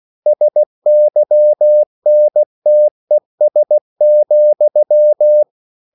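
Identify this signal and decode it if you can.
Morse code, heard as a keyed tone, about 16 wpm, 600 Hz. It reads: SYNTES,
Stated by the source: Slow (QRS) Morse code at 16 wpm, from CW med Gnister.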